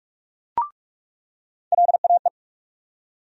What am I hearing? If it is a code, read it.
Morse code, heard as a keyed tone, about 46 wpm, 700 Hz. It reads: LAE
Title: USA Call Sign Postfixes - 3 Letters 45wpm